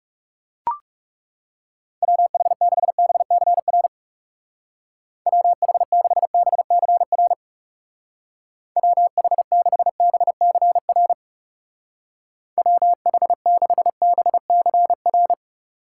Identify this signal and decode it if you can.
Morse code, heard as a keyed tone, about 35 wpm, 700 Hz. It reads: WH6BCR WH6BCR WH6BCR WH6BCR
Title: Speed Racing - USA 2x3 Call Signs 30wpm